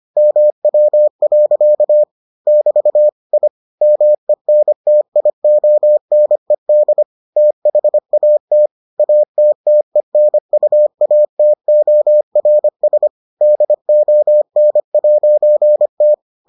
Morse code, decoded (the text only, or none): MW. <BT> I MENTIONED THAT ATTENUATORS DON'T